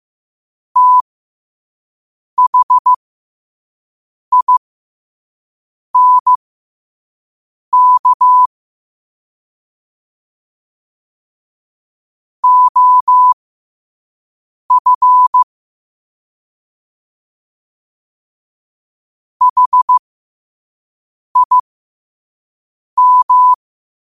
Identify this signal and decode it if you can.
Morse code, heard as a keyed tone, about 15 wpm, spaced as for 5 wpm, 1000 Hz. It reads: THINK OF HIM
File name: qotdfiles / 5wpm